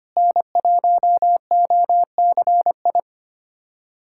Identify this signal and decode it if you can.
Morse code, heard as a keyed tone, about 25 wpm, 700 Hz. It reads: N1OCI
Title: USA Callsign DE Callsign 25wpm